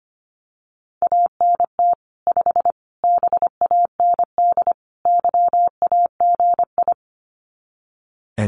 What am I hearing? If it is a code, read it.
Morse code, heard as a keyed tone, about 25 wpm, 700 Hz. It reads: ANT 5 BAND YAGI
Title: Common QSO Phrases 25wpm